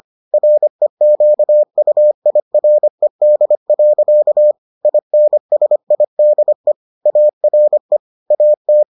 Morse code, heard as a keyed tone, about 25 wpm, 600 Hz. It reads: REQUIRED. INSIDE ARE AT